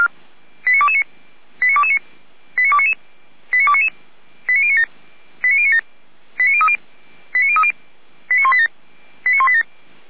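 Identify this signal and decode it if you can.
Five-tone selective call (ZVEI), heard as quick runs of five tones, tones of about 70 ms. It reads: 89229 89229 89299 89299 89097 89097 89030 89030 89187 89187
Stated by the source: ZVEI five-tone selective call received on SDR, Feuerwehr Orlamünde fire brigade.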